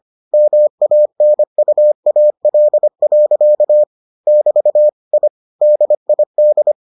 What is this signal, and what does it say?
Morse code, heard as a keyed tone, about 25 wpm, 600 Hz. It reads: MANUAL. <BT> I DID